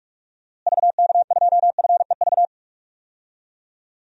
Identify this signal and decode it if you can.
Morse code, heard as a keyed tone, about 46 wpm, 700 Hz. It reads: VK2FEV
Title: DXCC-100 Call Signs - 6 Characters - Repeated 2x - 45wpm